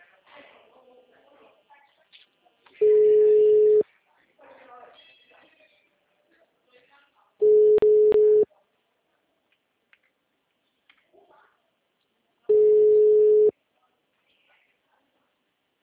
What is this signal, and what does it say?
Ringback tone (Europe, 425 Hz), with speech.